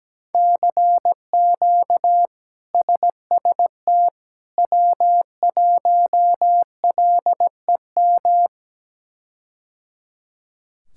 Morse code, heard as a keyed tone, about 17 wpm, 700 Hz. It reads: CQ SST W1LEM
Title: K1USN SST QSO Practice - 17wpm